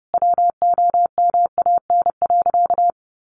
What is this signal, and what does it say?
Morse code, heard as a keyed tone, about 30 wpm, 700 Hz. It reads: WOMAN.
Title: Sentences from Top 1400 Words 30wpm